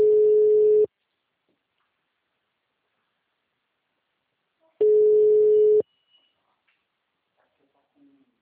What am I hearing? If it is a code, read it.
Ringback tone (Europe, 425 Hz), with speech.